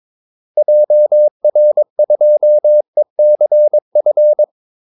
Morse code, heard as a keyed tone, about 22 wpm, 600 Hz. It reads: JR2ECF
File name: Callsigns-22wpm